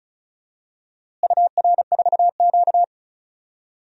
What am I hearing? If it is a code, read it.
Morse code, heard as a keyed tone, about 36 wpm, 700 Hz. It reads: UR4Q
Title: DXCC-100 Call Signs - 4 Characters - Repeated 2x - 35wpm